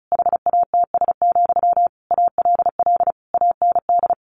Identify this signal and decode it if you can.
Morse code, heard as a keyed tone, about 34 wpm, 700 Hz. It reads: HATS, ALL AND
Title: A Sound Of Thunder 35wpm